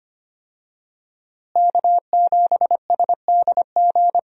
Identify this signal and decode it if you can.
Morse code, heard as a keyed tone, about 25 wpm, 700 Hz. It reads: K7SDG
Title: USA Call Signs - 5 Characters - Repeated 3x - 25wpm